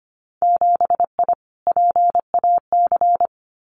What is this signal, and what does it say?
Morse code, heard as a keyed tone, about 25 wpm, 700 Hz. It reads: 7I PAC
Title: Winter Field Day Exchanges Encore 25wpm